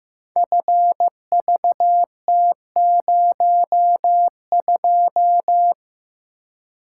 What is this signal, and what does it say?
Morse code, heard as a keyed tone, about 15 wpm, 700 Hz. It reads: FVT02
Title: Code Groups - 5 Letter:Numbers - 15wpm